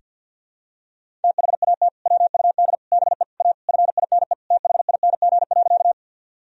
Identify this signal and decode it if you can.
Morse code, heard as a keyed tone, about 53 wpm, 700 Hz. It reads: THAT WUD BE A FINE THING.
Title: Sentences from Top 500 Words and Abbreviations 50wpm